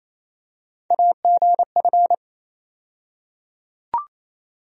Morse code, heard as a keyed tone, about 28 wpm, 700 Hz. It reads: AGF
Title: USA Call Sign Postfixes - 3 Letters 28wpm